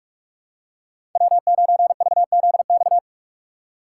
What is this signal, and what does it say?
Morse code, heard as a keyed tone, about 46 wpm, 700 Hz. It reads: W9VZX
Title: USA Call Signs - 5 Characters - 4x Character Spacing - 45wpm